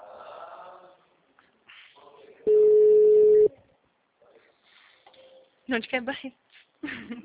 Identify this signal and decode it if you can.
Ringback tone (Europe, 425 Hz), with speech.